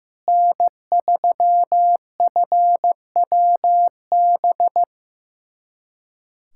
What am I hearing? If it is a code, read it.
Morse code, heard as a keyed tone, about 15 wpm, 700 Hz. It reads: N3FWB
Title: USA Callsign DE Callsign - 5 Characters - 15wpm